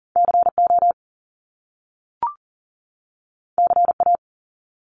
Morse code, heard as a keyed tone, about 42 wpm, 700 Hz. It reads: CO CA
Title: US States 40wpm